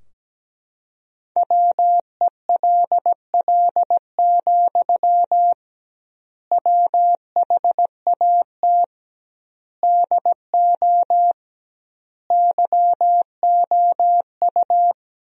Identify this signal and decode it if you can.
Morse code, heard as a keyed tone, about 17 wpm, 700 Hz. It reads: WELL, WHAT DO YOU
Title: 2x Word-Spacing - Sentences from Top 200 Words - 17wpm